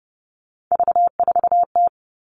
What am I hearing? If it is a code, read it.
Morse code, heard as a keyed tone, about 30 wpm, 700 Hz. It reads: V4T
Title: All Cut Numbers - 3-Digit 30wpm